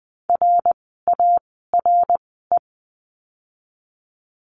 Morse code, heard as a keyed tone, about 20 wpm, 700 Hz. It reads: R A R E
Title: 061 Lesson 11f - Letter h - Words 20wpm